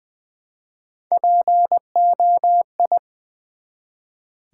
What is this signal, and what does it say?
Morse code, heard as a keyed tone, about 20 wpm, 700 Hz. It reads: POI
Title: Build A Word - 1 Letter At A Time - 3 to 5-Letter Words - 20wpm